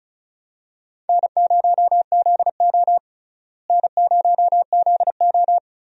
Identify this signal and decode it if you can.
Morse code, heard as a keyed tone, about 34 wpm, 700 Hz. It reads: N0ZO N0ZO
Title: USA Call Signs - 4 Characters - Repeated 3x - 35wpm